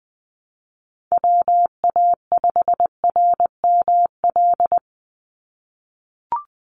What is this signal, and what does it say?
Morse code, heard as a keyed tone, about 20 wpm, 700 Hz. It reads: WA5RML
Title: Call Signs - 20wpm